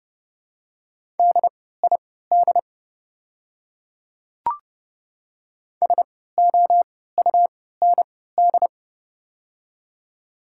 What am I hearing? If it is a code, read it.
Morse code, heard as a keyed tone, about 30 wpm, spaced as for 17 wpm, 700 Hz. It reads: DID SOUND
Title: 3x Character Spacing - Top 100 Words - 30wpm